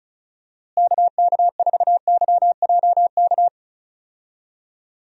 Morse code, heard as a keyed tone, about 34 wpm, 700 Hz. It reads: KK4YJK
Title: USA Call Signs - 6 Characters - Repeated 2x - 35wpm